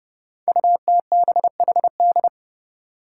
Morse code, heard as a keyed tone, about 30 wpm, 700 Hz. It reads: UTBHD
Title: Five Challenging Letter-Number Combinations 30wpm